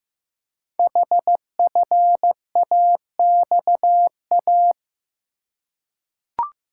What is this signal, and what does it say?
Morse code, heard as a keyed tone, about 15 wpm, 700 Hz. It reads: HFAXA
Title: Code Groups - 5 Letters - 15wpm